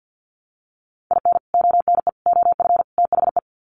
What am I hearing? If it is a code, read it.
Morse code, heard as a keyed tone, about 50 wpm, 700 Hz. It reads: IN ONE OF THE
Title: Sets of 4 Words 50wpm